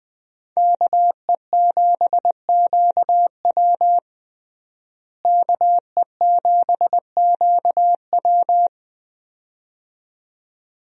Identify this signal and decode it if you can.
Morse code, heard as a keyed tone, about 20 wpm, 700 Hz. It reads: KE7QW KE7QW
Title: USA Call Signs - 5 Characters - Repeated 2x - 20wpm